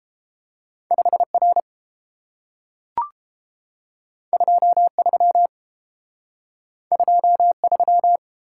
Morse code, heard as a keyed tone, about 30 wpm, 700 Hz. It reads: 5 R 2 3 2 3